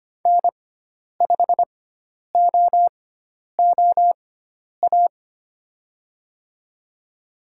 Morse code, heard as a keyed tone, about 25 wpm, spaced as for 10 wpm, 700 Hz. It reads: N5OOA
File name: Lesson 16g - Letter c - Call Signs 25wpm